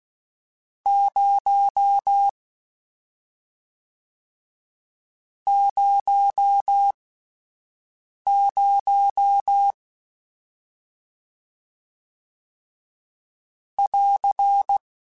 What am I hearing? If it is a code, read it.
Morse code, heard as a keyed tone, about 16 wpm, spaced as for 5 wpm, 800 Hz. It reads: 0 00 <AR>